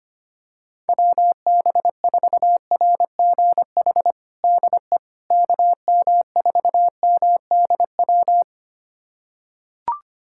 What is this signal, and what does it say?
Morse code, heard as a keyed tone, about 25 wpm, 700 Hz. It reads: WB4RGH DE KM4MDW
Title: USA Callsign DE Callsign 25wpm